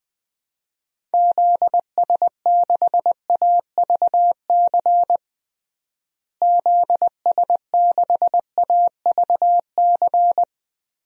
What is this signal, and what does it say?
Morse code, heard as a keyed tone, about 21 wpm, 700 Hz. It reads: ZS6AVC ZS6AVC
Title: DXCC-100 Call Signs - 6 Characters - Repeated 3x - 20wpm